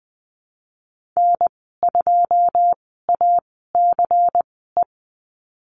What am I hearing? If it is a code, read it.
Morse code, heard as a keyed tone, about 20 wpm, spaced as for 15 wpm, 700 Hz. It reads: N2ACE
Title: 186 Lesson 30g - Letter b - Call Signs 20wpm